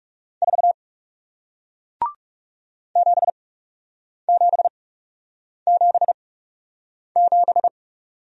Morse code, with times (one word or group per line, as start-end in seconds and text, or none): 0.4-0.7 4
3.0-7.7 7777